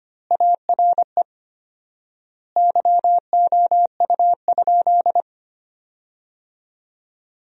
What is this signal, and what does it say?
Morse code, heard as a keyed tone, about 25 wpm, 700 Hz. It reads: ARE YOU?
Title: 4x Word-Spacing - Sentences from Top 100 Words - 25wpm